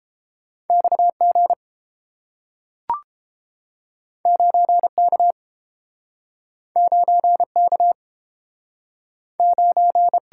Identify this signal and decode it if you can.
Morse code, heard as a keyed tone, about 30 wpm, 700 Hz. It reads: XG 9K 9K 9